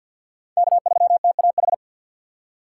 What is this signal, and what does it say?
Morse code, heard as a keyed tone, about 50 wpm, 690 Hz. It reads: K3TAH
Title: USA Call Signs - 3x Character Spacing - 50wpm